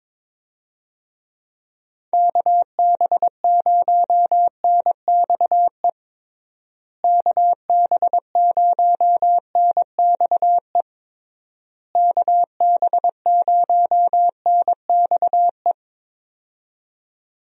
Morse code, heard as a keyed tone, about 22 wpm, 700 Hz. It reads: KB0NXE KB0NXE KB0NXE
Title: USA Call Signs - 6 Characters - Repeated 3x - 22wpm